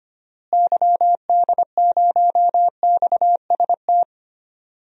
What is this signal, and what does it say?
Morse code, heard as a keyed tone, about 25 wpm, 700 Hz. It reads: YD0XST
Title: DXCC-100 Call Signs - 6 Characters - Repeated 3x - 25wpm